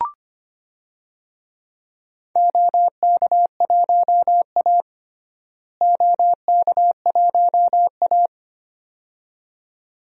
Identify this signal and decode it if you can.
Morse code, heard as a keyed tone, about 25 wpm, 700 Hz. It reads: OK1A OK1A
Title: DXCC-100 Call Signs - 4 Characters - Repeated 2x - 25wpm